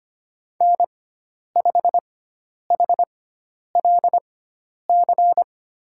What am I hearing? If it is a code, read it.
Morse code, heard as a keyed tone, about 25 wpm, spaced as for 10 wpm, 700 Hz. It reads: N5HLC
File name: Lesson 16g - Letter c - Call Signs 25wpm